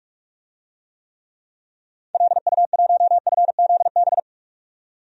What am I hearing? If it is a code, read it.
Morse code, heard as a keyed tone, about 46 wpm, 700 Hz. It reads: LU1FZB
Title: DXCC-100 Call Signs - 6 Characters - 45wpm